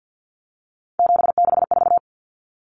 Morse code, heard as a keyed tone, about 50 wpm, 700 Hz. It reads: 764
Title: Numbers - 3-Digit 50wpm